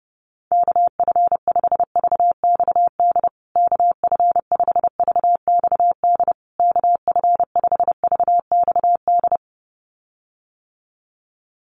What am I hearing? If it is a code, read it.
Morse code, heard as a keyed tone, about 30 wpm, 700 Hz. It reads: KF5VXD KF5VXD KF5VXD
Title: USA Call Signs Repeated 3x 30wpm